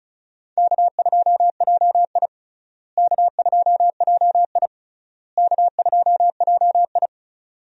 Morse code, heard as a keyed tone, about 36 wpm, 700 Hz. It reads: K2JI K2JI K2JI
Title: USA Call Signs - 4 Characters - Repeated 3x - 35wpm